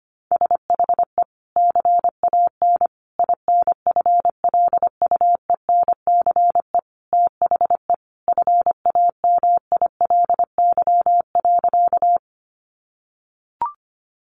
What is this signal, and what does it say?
Morse code, heard as a keyed tone, about 25 wpm, 700 Hz. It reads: SHE CAN INFLUENCE THE FAMILY.
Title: Sentences from Top 1100 Words 25wpm